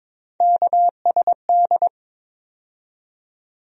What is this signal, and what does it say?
Morse code, heard as a keyed tone, about 22 wpm, 700 Hz. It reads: KSD
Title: USA Call Sign Postfixes - 3 Letters 22wpm